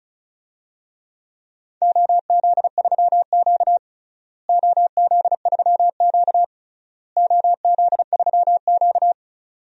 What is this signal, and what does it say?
Morse code, heard as a keyed tone, about 36 wpm, 700 Hz. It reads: OZ3Q OZ3Q OZ3Q
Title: DXCC-100 Call Signs - 4 Characters - Repeated 3x - 35wpm